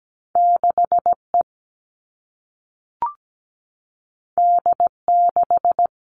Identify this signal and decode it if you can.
Morse code, heard as a keyed tone, about 17 wpm, 700 Hz. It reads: 6E D6
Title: All Cut Numbers - 2-Digit 17wpm